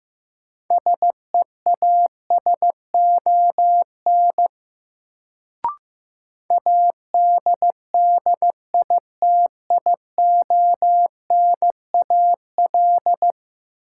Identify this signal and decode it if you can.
Morse code, heard as a keyed tone, about 15 wpm, 700 Hz. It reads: SEASON ADDITIONAL
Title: Sentences from Top 1400 Words - New Word Drill Alt - 15wpm